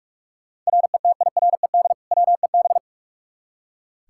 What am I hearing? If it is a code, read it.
Morse code, heard as a keyed tone, about 46 wpm, 700 Hz. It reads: RETIRED WEB
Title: Practice QSO - Exchange 3 - More Info - 45wpm